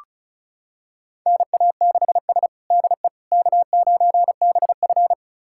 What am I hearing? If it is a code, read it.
Morse code, heard as a keyed tone, about 36 wpm, 700 Hz. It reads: NA6S DE K9BF
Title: USA Callsign DE Callsign - 4 Characters - 35wpm